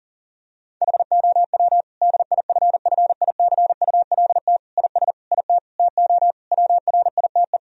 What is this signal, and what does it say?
Morse code, heard as a keyed tone, about 40 wpm, 700 Hz. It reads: HOW DIFFICULT IS IT TO WRITE
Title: Sentences from Top 1000 Words - Review Part 3 - 40wpm